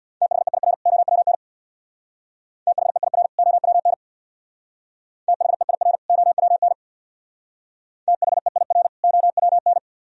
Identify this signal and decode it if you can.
Morse code, heard as a keyed tone, about 63 wpm, 700 Hz. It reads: THEIR OWN THEIR OWN THEIR OWN THEIR OWN